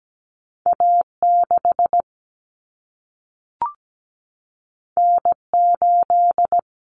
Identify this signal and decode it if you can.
Morse code, heard as a keyed tone, about 17 wpm, 700 Hz. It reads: A6 N8
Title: Common Cut Numbers - 2-Digit 17wpm